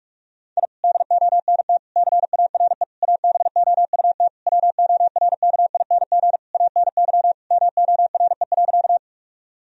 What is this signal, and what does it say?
Morse code, heard as a keyed tone, about 46 wpm, 700 Hz. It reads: I DONT CARE ABOUT WORKING ANY MORE.
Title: Sentences from Top 1000 Words - Review Part 4 - 45wpm